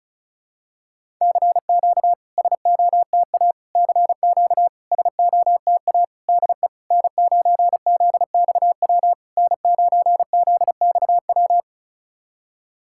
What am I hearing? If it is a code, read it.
Morse code, heard as a keyed tone, about 36 wpm, 700 Hz. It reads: CQ SOTA CQ SOTA DE N9ZXW N9ZXW
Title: SOTA QSO Practice 35wpm